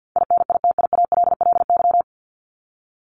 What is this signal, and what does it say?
Morse code, heard as a keyed tone, about 50 wpm, 700 Hz. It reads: INITIALLY
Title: Top 1251-1500 Words 50wpm